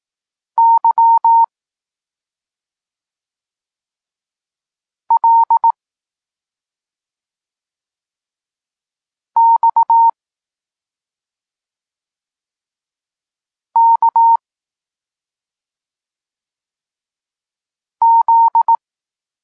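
Morse code, heard as a keyed tone, about 18 wpm, 920 Hz. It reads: Y L X K Z
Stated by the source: Polish Morse lesson teaching the letter C, from SP9MOA Niepołomicki Klub Krótkofalowców.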